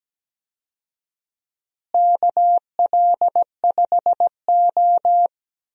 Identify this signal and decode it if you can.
Morse code, heard as a keyed tone, about 17 wpm, 700 Hz. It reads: KL5O
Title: USA Call Signs - 4 Characters - Repeated 3x - 17wpm